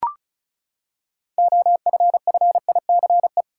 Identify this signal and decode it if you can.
Morse code, heard as a keyed tone, about 36 wpm, 700 Hz. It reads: OFFICE
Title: Six Letter Words - 35wpm